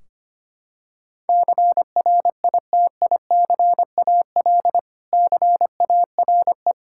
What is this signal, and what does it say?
Morse code, heard as a keyed tone, about 25 wpm, 700 Hz. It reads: CRITICAL CARE
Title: Practice QSO - Exchange 3 - More Info - 25wpm